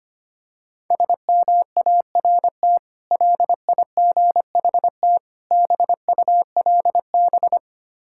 Morse code, heard as a keyed tone, about 25 wpm, 700 Hz. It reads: SMART LIGHT BULB
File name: practice / 25wpm / Difficult Words with Hints - 25wpm